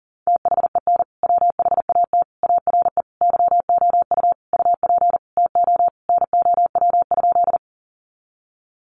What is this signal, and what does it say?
Morse code, heard as a keyed tone, about 40 wpm, 700 Hz. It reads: THEN WHAT ARE YOU UP TO NOW?
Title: Sentences from Top 100 Words 40wpm